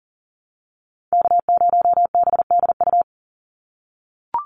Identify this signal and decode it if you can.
Morse code, heard as a keyed tone, about 40 wpm, 700 Hz. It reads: K0BDU
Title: USA Call Signs 40wpm